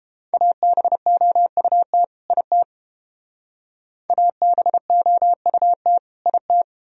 Morse code, heard as a keyed tone, about 29 wpm, 700 Hz. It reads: ABOUT IT ABOUT IT